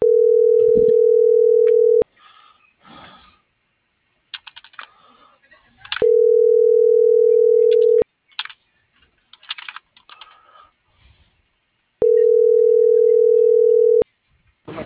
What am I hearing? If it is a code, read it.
Ringback tone (North America), 440 + 480 Hz.